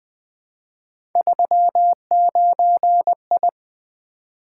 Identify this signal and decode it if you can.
Morse code, heard as a keyed tone, about 20 wpm, 700 Hz. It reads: 39I
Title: Three Letter-Number Combinations - 20wpm